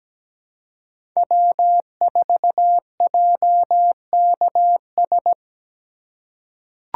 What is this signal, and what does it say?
Morse code, heard as a keyed tone, about 17 wpm, 700 Hz. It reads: W4JKS
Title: USA Call Signs - 5 Characters - 3x Character Spacing - 17wpm